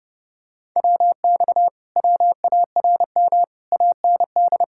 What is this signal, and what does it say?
Morse code, heard as a keyed tone, about 31 wpm, 700 Hz. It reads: WX WARM AND